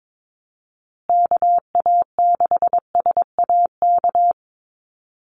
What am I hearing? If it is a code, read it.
Morse code, heard as a keyed tone, about 22 wpm, 700 Hz. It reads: KA6SAK
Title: USA Call Signs Repeated 2x 22wpm